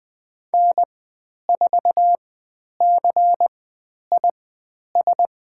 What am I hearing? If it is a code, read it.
Morse code, heard as a keyed tone, about 20 wpm, spaced as for 10 wpm, 700 Hz. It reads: N4CIS
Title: 102 Lesson 17g - Letter u - Call Signs 20wpm